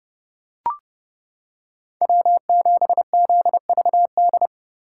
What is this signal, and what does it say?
Morse code, heard as a keyed tone, about 29 wpm, 700 Hz. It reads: W7ZVD